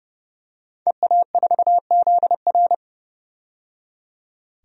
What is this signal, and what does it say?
Morse code, heard as a keyed tone, about 32 wpm, 700 Hz. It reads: EA4ZR
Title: DXCC-100 Call Signs - 5 Characters - 30wpm